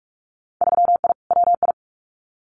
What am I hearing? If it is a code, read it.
Morse code, heard as a keyed tone, about 46 wpm, 700 Hz. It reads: 3I WI
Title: Winter Field Day Exchanges 45wpm